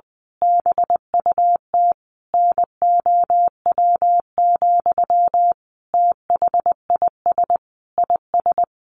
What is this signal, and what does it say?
Morse code, heard as a keyed tone, about 20 wpm, 700 Hz. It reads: BUT NOW, THIS IS
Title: A Sound Of Thunder 20wpm